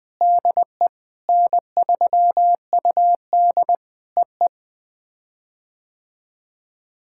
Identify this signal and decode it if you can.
Morse code, heard as a keyed tone, about 18 wpm, 700 Hz. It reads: DE N3UD EE
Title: POTA QSO Practice - 20wpm